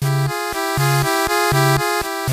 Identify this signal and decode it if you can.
There is no code.